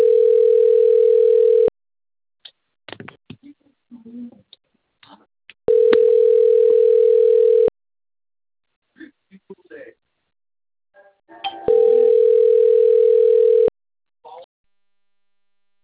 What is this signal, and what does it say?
Ringback tone (North America), 440 + 480 Hz, with speech.
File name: out- - - - - .46686